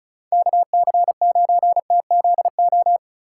Morse code, heard as a keyed tone, about 34 wpm, 700 Hz. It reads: KC9TZO
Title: K1USN SST QSO Practice - 35wpm